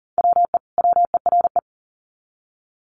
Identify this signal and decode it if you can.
Morse code, heard as a keyed tone, about 40 wpm, 700 Hz. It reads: WE WERE
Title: Sets of 2 Words 40wpm